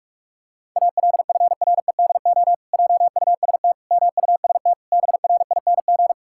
Morse code, heard as a keyed tone, about 41 wpm, 700 Hz. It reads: ALFREDO JUST MUST BRING
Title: Pangram Challenge v2 - 45wpm